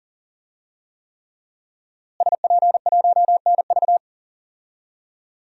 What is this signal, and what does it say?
Morse code, heard as a keyed tone, about 40 wpm, 700 Hz. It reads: SP1NV
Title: DXCC-100 Call Signs - 5 Characters - Repeated 3x - 40wpm